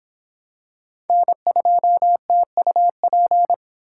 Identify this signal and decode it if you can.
Morse code, heard as a keyed tone, about 26 wpm, 700 Hz. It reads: N2TUP